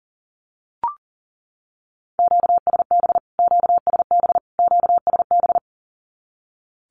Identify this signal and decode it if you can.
Morse code, heard as a keyed tone, about 40 wpm, 700 Hz. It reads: QSB QSB QSB
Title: Q-Codes Repeated 3x 40wpm